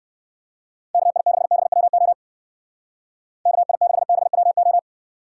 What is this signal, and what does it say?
Morse code, heard as a keyed tone, about 59 wpm, 700 Hz. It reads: KI6BWO KI6BWO